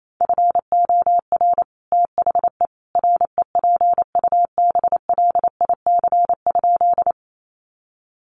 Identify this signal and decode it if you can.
Morse code, heard as a keyed tone, about 28 wpm, 700 Hz. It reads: FOR THE REPUBLIC?
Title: Sentences from Top 1500 Words 28wpm